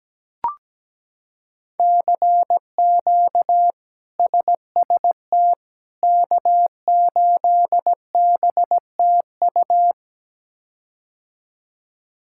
Morse code, heard as a keyed tone, about 17 wpm, 700 Hz. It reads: CQ SST K8BTU
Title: K1USN SST QSO Practice - 17wpm